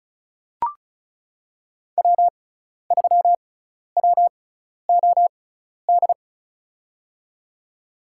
Morse code, heard as a keyed tone, about 34 wpm, spaced as for 12 wpm, 700 Hz. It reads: W3WOD
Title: USA Call Signs - 6x Character Spacing - 35wpm